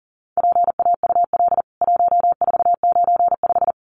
Morse code, heard as a keyed tone, about 40 wpm, 700 Hz. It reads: PAUL 1495
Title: CWT Exchange 40wpm